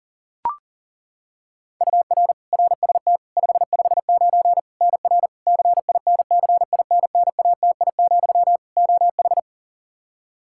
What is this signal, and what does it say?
Morse code, heard as a keyed tone, about 40 wpm, 700 Hz. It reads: UR RST 559 NR CINCINNATI, OH
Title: K1USN SST QSO Practice 40wpm